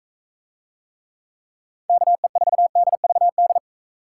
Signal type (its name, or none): Morse code